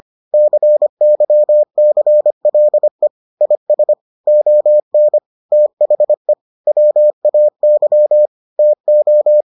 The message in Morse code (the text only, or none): CYCLE IS ON THE WAY TO